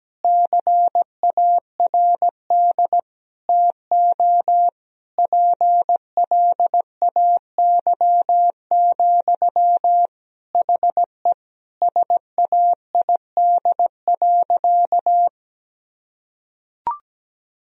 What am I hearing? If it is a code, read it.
Morse code, heard as a keyed tone, about 17 wpm, 700 Hz. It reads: CARD TO PLAY, HE SAID.
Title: Sentences from Top 1000 Words - Review Part 4 - 17wpm